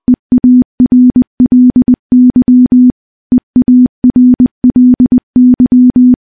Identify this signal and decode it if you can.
Morse code, heard as a keyed tone, about 20 wpm, 250 Hz. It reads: EARLY EARLY
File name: morse-training-20WPM